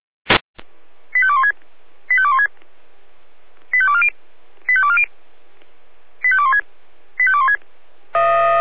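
Five-tone selective call (ZVEI), heard as quick runs of five tones, tones of about 70 ms. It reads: 85216 85216 85240 85240 85216 85216
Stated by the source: ZVEI five-tone call received on SDR.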